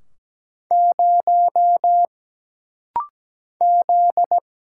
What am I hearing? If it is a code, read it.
Morse code, heard as a keyed tone, about 17 wpm, 700 Hz. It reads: 0 Z
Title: Single Letter-Number Alt - Rapid-Fire 17wpm